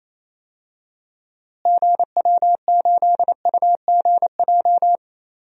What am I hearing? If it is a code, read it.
Morse code, heard as a keyed tone, about 27 wpm, 700 Hz. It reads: GW8UGJ
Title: DXCC-100 Call Signs - 6 Characters - Repeated 3x - 28wpm